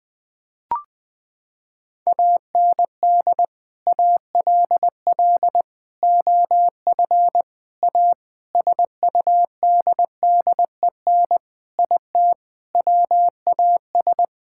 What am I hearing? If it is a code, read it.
Morse code, heard as a keyed tone, about 21 wpm, 700 Hz. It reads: AND ALL OF A SUDDEN IT WAS
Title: Sentences with 5-Word Phrases - 20wpm